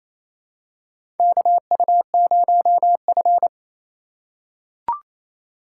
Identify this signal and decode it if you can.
Morse code, heard as a keyed tone, about 28 wpm, 700 Hz. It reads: KU0F
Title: USA Call Signs - 4x Character Spacing - 28wpm